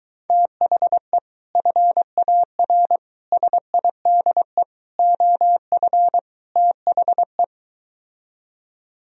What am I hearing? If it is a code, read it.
Morse code, heard as a keyed tone, about 23 wpm, 700 Hz. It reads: THE FAR SIDE OF THE